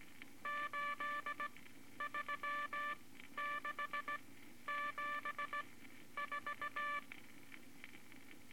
Morse code, heard as a keyed tone, about 17 wpm, spaced as for 11 wpm, 1170 Hz, with noise. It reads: 83674
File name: Morse Code Records / 18 B LCM 165 Ten Words Per Minute